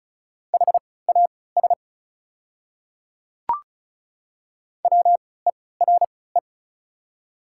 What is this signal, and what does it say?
Morse code, heard as a keyed tone, about 34 wpm, spaced as for 20 wpm, 690 Hz. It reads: HAS WERE
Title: 3x Character Spacing - Top 300 Words - 35wpm